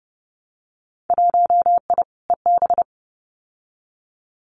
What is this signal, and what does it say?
Morse code, heard as a keyed tone, about 30 wpm, 700 Hz. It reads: 1I EB
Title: Winter Field Day Exchanges Encore 30wpm